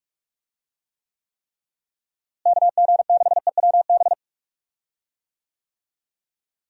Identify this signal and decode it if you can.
Morse code, heard as a keyed tone, about 46 wpm, 700 Hz. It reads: KG6EWB
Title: USA Call Signs - 6 Characters - Repeated 3x - 45wpm